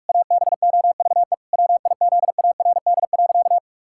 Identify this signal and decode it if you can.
Morse code, heard as a keyed tone, about 46 wpm, 700 Hz. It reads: ABOVE WIZARD.
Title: Pangram Challenge v2 - 45wpm